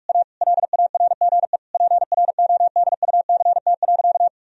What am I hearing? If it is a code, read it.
Morse code, heard as a keyed tone, about 44 wpm, 700 Hz. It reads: A LARGE PRODUCT.
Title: Sentences from Top 1000 Words - Review Part 2 - 45wpm